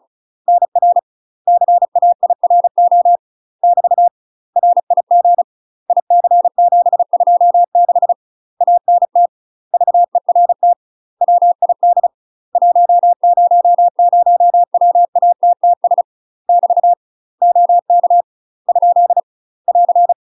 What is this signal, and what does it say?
Morse code, heard as a keyed tone, about 34 wpm, 700 Hz. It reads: NR CAIRO <BT> RIG IC726 ANT VERT WID 100WATTS <BT> OK ? <AR>
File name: UK MASTER QSOs TRAINING / 35WPM / UK MASTER QSOs 0003